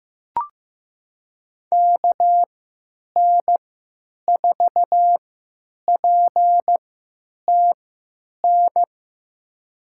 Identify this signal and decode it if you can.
Morse code, heard as a keyed tone, about 15 wpm, spaced as for 9 wpm, 700 Hz. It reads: KN4PTN
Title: USA Call Signs - 6 Characters - 3x Character Spacing - 15wpm